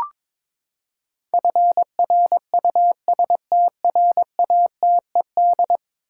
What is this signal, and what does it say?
Morse code, heard as a keyed tone, about 22 wpm, 700 Hz. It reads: FRUSTRATED
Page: morse-code-ninja-practice-22wpm